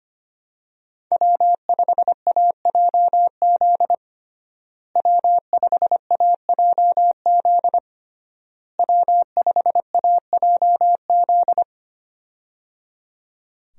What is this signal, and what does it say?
Morse code, heard as a keyed tone, about 25 wpm, 700 Hz. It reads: W5AJZ W5AJZ W5AJZ
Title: USA Call Signs - 5 Characters - Repeated 3x - 25wpm